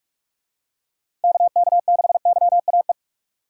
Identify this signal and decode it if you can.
Morse code, heard as a keyed tone, about 44 wpm, 700 Hz. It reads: KK6YAE